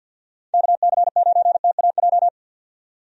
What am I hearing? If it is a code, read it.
Morse code, heard as a keyed tone, about 50 wpm, 700 Hz. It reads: KC9TAJ
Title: USA Call Signs - 6 Characters - 50wpm